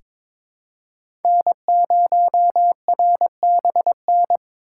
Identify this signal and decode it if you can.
Morse code, heard as a keyed tone, about 22 wpm, 700 Hz. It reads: N0RBN
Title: USA Call Signs - 5 Characters - 2x Character Spacing - 22wpm